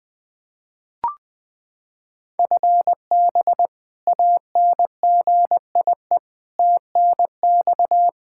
Morse code, heard as a keyed tone, about 20 wpm, 700 Hz. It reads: FB ANGIE TNX
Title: K1USN SST QSO Practice 20wpm